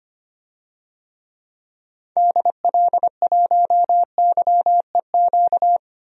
Morse code, heard as a keyed tone, about 25 wpm, 700 Hz. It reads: DL1YEQ
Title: DXCC-100 Call Signs - 6 Characters - 25wpm